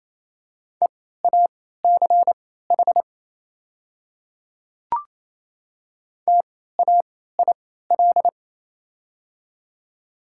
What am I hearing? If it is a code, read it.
Morse code, heard as a keyed tone, about 27 wpm, spaced as for 16 wpm, 700 Hz. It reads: EACH TAIL
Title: 3x Character Spacing - Top 500 Words - 28wpm